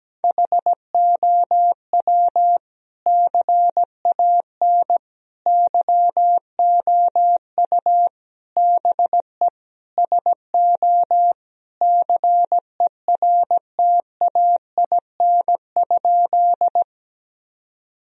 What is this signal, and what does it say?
Morse code, heard as a keyed tone, about 17 wpm, 700 Hz. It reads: HOW CAN YOU BE SO CERTAIN?
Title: Sentences from Top 1000 Words - Review Part 2 - 17wpm